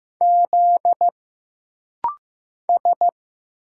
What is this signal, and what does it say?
Morse code, heard as a keyed tone, about 15 wpm, 700 Hz. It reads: Z S
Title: Single Letters - Rapid-Fire 15wpm